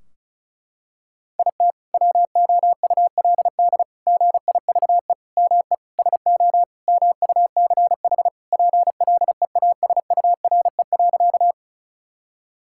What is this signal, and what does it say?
Morse code, heard as a keyed tone, about 36 wpm, 700 Hz. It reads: IT WOULD GIVE ME SO MUCH PLEASURE.
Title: Sentences from Top 1600 Words 35wpm